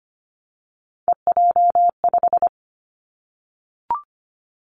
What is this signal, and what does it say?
Morse code, heard as a keyed tone, about 25 wpm, 700 Hz. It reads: EJ5
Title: Three Challenging Letter-Number Combinations 25wpm